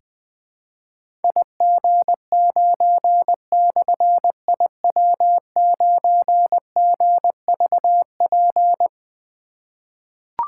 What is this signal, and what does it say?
Morse code, heard as a keyed tone, about 21 wpm, 700 Hz. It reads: IG9/IW9GVP
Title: Difficult Call Signs 20wpm